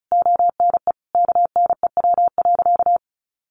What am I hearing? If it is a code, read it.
Morse code, heard as a keyed tone, about 34 wpm, 700 Hz. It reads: ONE KNEW.
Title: Sentences from Top 400 Words 35wpm